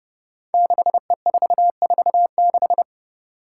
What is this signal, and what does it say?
Morse code, heard as a keyed tone, about 30 wpm, 700 Hz. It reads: 6E446
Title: Five Challenging Letter-Number Combinations 30wpm